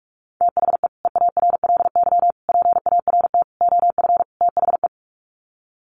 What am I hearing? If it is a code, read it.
Morse code, heard as a keyed tone, about 44 wpm, 690 Hz. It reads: THE EARLY PART OF THE
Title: Sets of 5 Words Encore2 45wpm